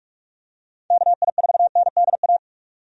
Morse code, heard as a keyed tone, about 46 wpm, 700 Hz. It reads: KI4NDA
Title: USA Call Signs - 6 Characters - 45wpm